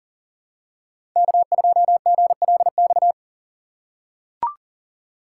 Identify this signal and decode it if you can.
Morse code, heard as a keyed tone, about 40 wpm, 700 Hz. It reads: K2GLX
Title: USA Call Signs - 5 Characters - 3x Character Spacing - 40wpm